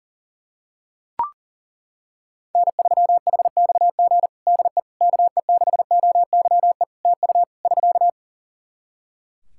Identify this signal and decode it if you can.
Morse code, heard as a keyed tone, about 40 wpm, 700 Hz. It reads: N3HXG DE KE6OYE TU <SK>
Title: Practice QSO - Exchange 4 - Ending - 40wpm